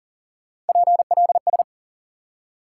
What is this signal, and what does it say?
Morse code, heard as a keyed tone, about 50 wpm, 700 Hz. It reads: PLS
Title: Speed-Racing Top 90 QSO Elements 40wpm